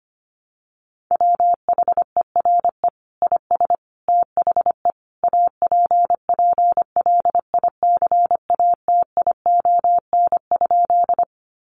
Morse code, heard as a keyed tone, about 25 wpm, 700 Hz. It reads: WHERE IS THE APPLICATION?
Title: Sentences from Top 1200 Words 25wpm